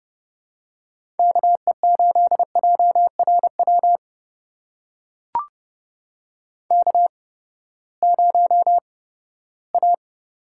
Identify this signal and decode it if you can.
Morse code, heard as a keyed tone, about 30 wpm, spaced as for 17 wpm, 700 Hz. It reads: KE8JRW K0A
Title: USA Call Signs - 8x Character Spacing - 30wpm